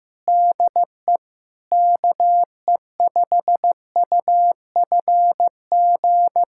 Morse code, heard as a keyed tone, about 15 wpm, 700 Hz. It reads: DE KE5UFG
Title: USA Callsign DE Callsign - 6 Characters - 15wpm